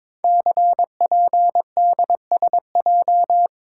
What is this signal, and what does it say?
Morse code, heard as a keyed tone, about 22 wpm, 700 Hz. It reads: CPDSJ
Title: Code Groups - 5 Letter:Numbers - 22wpm